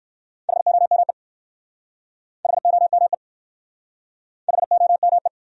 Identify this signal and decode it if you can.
Morse code, heard as a keyed tone, about 56 wpm, 700 Hz. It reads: HOME HOME HOME